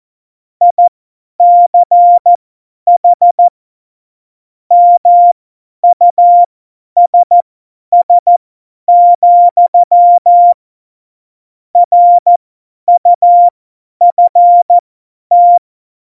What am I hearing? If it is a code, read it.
Morse code, heard as a keyed tone, about 14 wpm, spaced as for 10 wpm, 700 Hz. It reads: ICH MUSS, RUFT